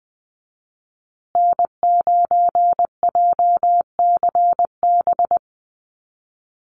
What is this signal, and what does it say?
Morse code, heard as a keyed tone, about 20 wpm, 700 Hz. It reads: N9JCB
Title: USA Call Signs Repeated 2x 20wpm